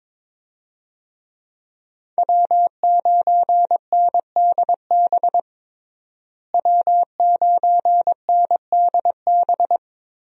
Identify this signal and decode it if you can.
Morse code, heard as a keyed tone, about 22 wpm, 700 Hz. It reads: W9NDB W9NDB
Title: USA Call Signs - 5 Characters - Repeated 3x - 22wpm